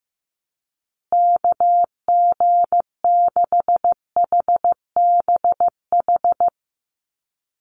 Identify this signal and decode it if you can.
Morse code, heard as a keyed tone, about 14 wpm, 700 Hz. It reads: KG6HBH
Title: USA Call Signs Repeated 2x 15wpm